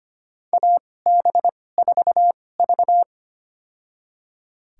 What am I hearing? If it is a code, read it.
Morse code, heard as a keyed tone, about 25 wpm, spaced as for 18 wpm, 700 Hz. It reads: AB4V
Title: USA Call Signs - 2x Character Spacing - 25wpm